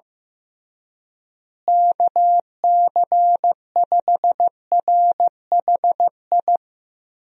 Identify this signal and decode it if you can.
Morse code, heard as a keyed tone, about 15 wpm, 700 Hz. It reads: KC5RHI
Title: USA Call Signs - 6 Characters - Repeated 3x - 15wpm